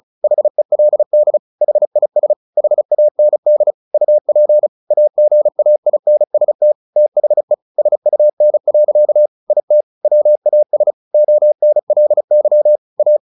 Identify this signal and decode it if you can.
Morse code, heard as a keyed tone, about 36 wpm, 600 Hz. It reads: HELD HIS HAND UP AGAINST THE SUN. IT WAS ONLY A
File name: the old man and the sea / the old man and the sea 1 35wpm